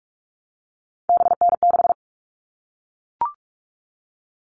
Morse code, heard as a keyed tone, about 44 wpm, 700 Hz. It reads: BN6
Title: All Cut Numbers - 3-Digit 45wpm